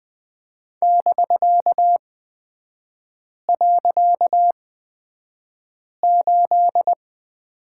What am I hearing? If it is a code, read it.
Morse code, heard as a keyed tone, about 20 wpm, spaced as for 5 wpm, 700 Hz. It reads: <BK>.8
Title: 255 Lesson 41e - Period - Three Characters 20wpm